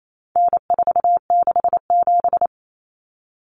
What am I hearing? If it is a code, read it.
Morse code, heard as a keyed tone, about 28 wpm, 700 Hz. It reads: N467
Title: Common Cut Numbers - 4-Digit 28wpm